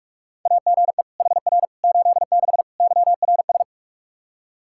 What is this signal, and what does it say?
Morse code, heard as a keyed tone, about 46 wpm, 700 Hz. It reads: AGE HR 86 YRS
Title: K1USN SST QSO Practice 45wpm